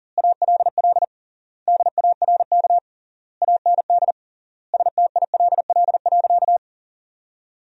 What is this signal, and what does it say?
Morse code, heard as a keyed tone, about 40 wpm, 700 Hz. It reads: ALL DARK AND STILL.
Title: 3x Word-Spacing - Sentences from Top 500 Words - 40wpm